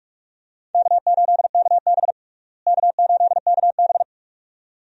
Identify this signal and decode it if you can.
Morse code, heard as a keyed tone, about 42 wpm, 700 Hz. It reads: K8KB K8KB
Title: USA Call Signs - 4 Characters - Repeated 2x - 45wpm